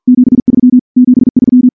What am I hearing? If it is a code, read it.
Morse code, heard as a keyed tone, about 47 wpm, 250 Hz. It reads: 73 73
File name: morse / morse-training-50WPM